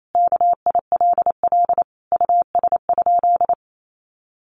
Morse code, heard as a keyed tone, about 28 wpm, 700 Hz. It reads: KILL US?
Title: Sentences from Top 1400 Words 28wpm